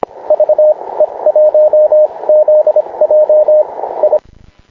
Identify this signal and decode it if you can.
Morse code, heard as a keyed tone, about 25 wpm, spaced as for 20 wpm, 610 Hz, with radio noise. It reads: VE1ZJ I
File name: klub / sat 0144 dj6qt ve1zj